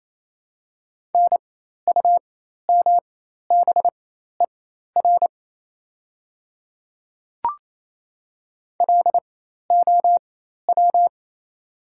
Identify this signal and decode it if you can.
Morse code, heard as a keyed tone, about 28 wpm, spaced as for 13 wpm, 700 Hz. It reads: NUMBER LOW
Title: 4x Character Spacing - Top 300 Words - 28wpm